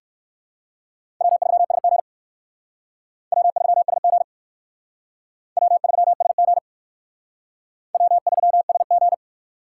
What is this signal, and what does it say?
Morse code, heard as a keyed tone, about 63 wpm, 700 Hz. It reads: W3SG W3SG W3SG W3SG